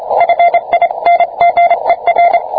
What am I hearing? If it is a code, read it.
Morse code, heard as a keyed tone, about 27 wpm, 690 Hz. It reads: FINGER